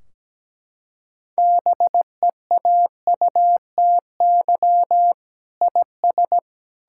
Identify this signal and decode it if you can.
Morse code, heard as a keyed tone, about 17 wpm, 700 Hz. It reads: BEAUTY IS
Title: Sentences from Top 1600 Words 17wpm